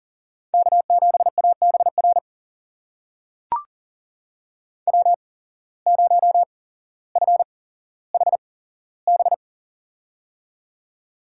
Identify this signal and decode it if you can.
Morse code, heard as a keyed tone, about 40 wpm, spaced as for 17 wpm, 700 Hz. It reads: K7ABR W0FHB
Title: USA Call Signs - 5 Characters - 8x Character Spacing - 40wpm